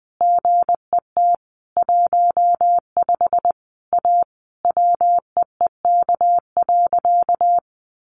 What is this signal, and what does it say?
Morse code, heard as a keyed tone, about 20 wpm, 700 Hz. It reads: GET 15 A WEEK.
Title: Easy Phrases 20wpm